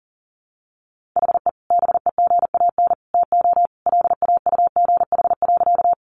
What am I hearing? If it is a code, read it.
Morse code, heard as a keyed tone, about 40 wpm, 700 Hz. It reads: HE BEGAN TO LAUGH.
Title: Sentences from Top 500 Words 40wpm